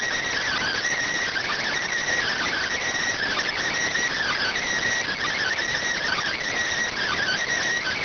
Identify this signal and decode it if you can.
Emergency Alert System (SAME) tones (header bursts), heard as harsh buzzing bursts. It is heard from 0.5 to 6.9 s.